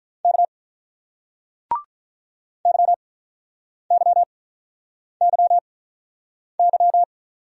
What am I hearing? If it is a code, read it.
Morse code, heard as a keyed tone, about 43 wpm, 700 Hz. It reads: K Y Y Y Y
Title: Speed-Racing Single Letters 35wpm